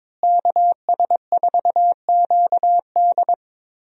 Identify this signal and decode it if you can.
Morse code, heard as a keyed tone, about 22 wpm, 700 Hz. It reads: KS4QD
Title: USA Call Signs - 5 Characters - Repeated 2x - 22wpm